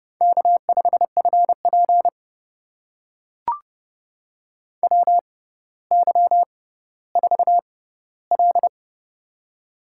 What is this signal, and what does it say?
Morse code, heard as a keyed tone, about 29 wpm, spaced as for 15 wpm, 700 Hz. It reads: K5FP WY4L
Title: USA Call Signs - 4 Characters - 6x Character Spacing - 30wpm